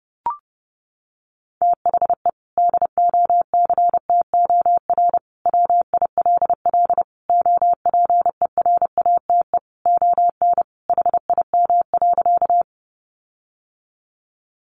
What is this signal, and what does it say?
Morse code, heard as a keyed tone, about 28 wpm, 700 Hz. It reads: THE DOCTOR WILL OPERATE ON HIM.
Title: Sentences from Top 1500 Words 30wpm